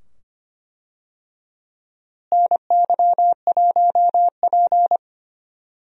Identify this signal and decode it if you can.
Morse code, heard as a keyed tone, about 25 wpm, 700 Hz. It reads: NY1P
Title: USA Call Signs - 4 Characters - Repeated 2x - 25wpm